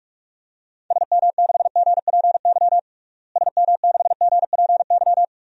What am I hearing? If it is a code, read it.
Morse code, heard as a keyed tone, about 46 wpm, 700 Hz. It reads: SM6GPY SM6GPY
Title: DXCC-100 Call Signs - 6 Characters - Repeated 3x - 45wpm